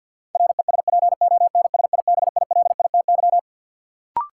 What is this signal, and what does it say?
Morse code, heard as a keyed tone, about 48 wpm, 700 Hz. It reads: RESPONSIBILITY
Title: Sentences from Top 1200 Words - New Word Drill - 50wpm